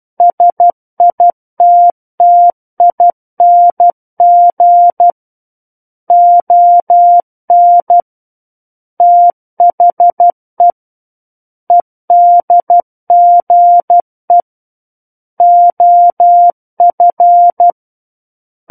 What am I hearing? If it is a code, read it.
Morse code, heard as a keyed tone, about 12 wpm, 700 Hz. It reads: SITTING ON THE EDGE OF
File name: Book One / 03-Chapter 3 - 12 WPM